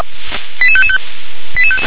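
Five-tone selective call (ZVEI), heard as quick runs of five tones, tones of about 70 ms. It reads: 88555
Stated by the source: ZVEI five-tone selective call received on an SDR.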